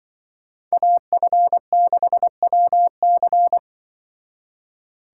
Morse code, heard as a keyed tone, about 23 wpm, 700 Hz. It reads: AF6WC
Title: Speed Racing - USA 2x2 Call Signs 20wpm